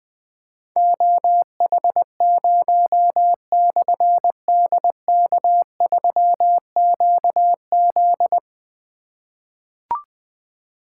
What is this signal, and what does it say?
Morse code, heard as a keyed tone, about 20 wpm, 700 Hz. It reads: OH0/DK3QZ
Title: Difficult Call Signs 20wpm